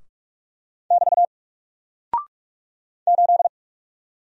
Morse code, heard as a keyed tone, about 44 wpm, 700 Hz. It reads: <BT> 8
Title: Single Character (Letter:Number:Punctuation:Prosign) 45wpm